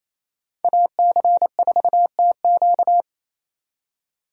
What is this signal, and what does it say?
Morse code, heard as a keyed tone, about 28 wpm, 700 Hz. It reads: AC4TQ
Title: USA Call Signs - 2x Character Spacing - 28wpm